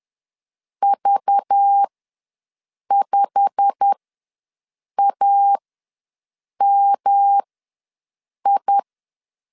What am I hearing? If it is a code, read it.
Morse code, heard as a keyed tone, about 10 wpm, spaced as for 6 wpm, 800 Hz. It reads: V5AMI
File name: LEZIONI / 10 SIGNS / 10D COMA ws 10wpm 4314